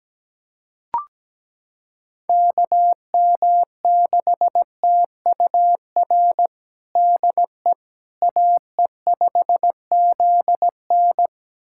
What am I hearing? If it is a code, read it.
Morse code, heard as a keyed tone, about 17 wpm, 700 Hz. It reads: KM6TUR DE AE5ZN
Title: USA Callsign DE Callsign 17wpm